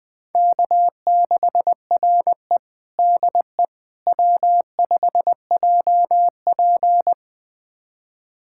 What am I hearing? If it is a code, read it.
Morse code, heard as a keyed tone, about 21 wpm, 700 Hz. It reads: K6RE DE W5JP
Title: USA Callsign DE Callsign - 4 Characters - 20wpm